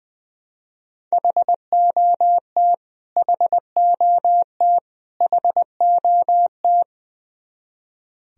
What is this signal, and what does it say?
Morse code, heard as a keyed tone, about 20 wpm, 700 Hz. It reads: HOT HOT HOT
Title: Top 100 Words Repeated 20wpm